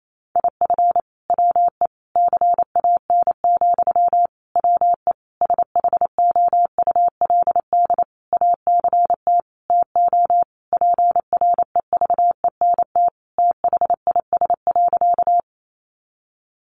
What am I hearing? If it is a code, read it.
Morse code, heard as a keyed tone, about 28 wpm, 700 Hz. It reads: IF WE CAN, WE SHOULD ACT TO PREVENT THIS.
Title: Sentences from Top 1100 Words 28wpm